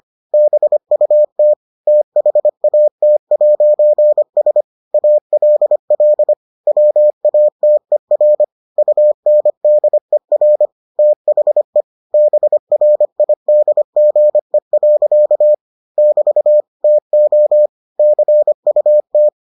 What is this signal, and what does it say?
Morse code, heard as a keyed tone, about 25 wpm, 600 Hz. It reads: BUT THAT'S ALL WATER UNDER THE BRIDGE. <BT> TO CUT